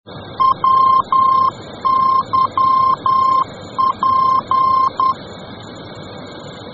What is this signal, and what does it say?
Morse code, heard as a keyed tone, about 10 wpm, 1050 Hz. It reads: WYP